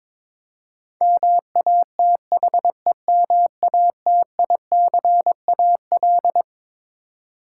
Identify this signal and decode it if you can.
Morse code, heard as a keyed tone, about 22 wpm, 700 Hz. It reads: MATHEMATICAL
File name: Twelve Letter Words - 22wpm